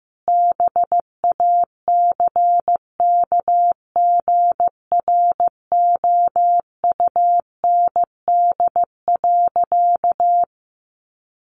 Morse code, heard as a keyed tone, about 15 wpm, 700 Hz. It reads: BACKGROUND.
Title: Sentences from Top 1300 Words 15wpm